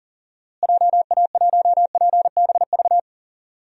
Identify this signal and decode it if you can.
Morse code, heard as a keyed tone, about 40 wpm, 700 Hz. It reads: JA1PBV
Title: DXCC-100 Call Signs - 6 Characters - Repeated 2x - 40wpm